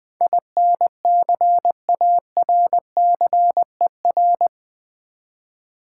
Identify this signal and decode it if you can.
Morse code, heard as a keyed tone, about 20 wpm, 700 Hz. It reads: INCARCER
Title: Build A Word - 4 Letters At A Time - 11-Letter Words - 20wpm